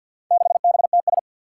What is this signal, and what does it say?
Morse code, heard as a keyed tone, about 47 wpm, 700 Hz. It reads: 6BTS
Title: Four Challenging Letter-Number Combinations 50wpm